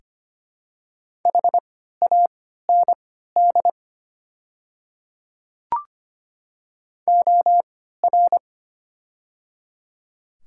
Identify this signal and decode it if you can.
Morse code, heard as a keyed tone, about 25 wpm, spaced as for 14 wpm, 700 Hz. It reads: HAND OR
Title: 3x Character Spacing - Top 200 Words - 25wpm